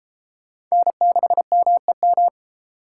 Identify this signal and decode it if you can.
Morse code, heard as a keyed tone, about 33 wpm, 700 Hz. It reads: N6MEM